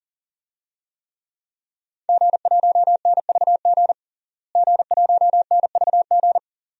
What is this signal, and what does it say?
Morse code, heard as a keyed tone, about 40 wpm, 700 Hz. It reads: G1NVG G1NVG
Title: DXCC-100 Call Signs - 5 Characters - Repeated 2x - 40wpm